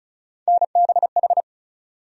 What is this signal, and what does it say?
Morse code, heard as a keyed tone, about 36 wpm, 700 Hz. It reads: NBH
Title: USA Call Sign Postfixes - 3 Letters 35wpm